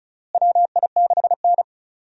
Morse code, heard as a keyed tone, about 34 wpm, 700 Hz. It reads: WI6N
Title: USA Call Signs - 4 Characters - Repeated 3x - 35wpm